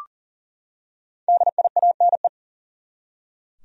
Morse code, heard as a keyed tone, about 40 wpm, 700 Hz. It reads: DIANE
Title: USA Common Names - 40wpm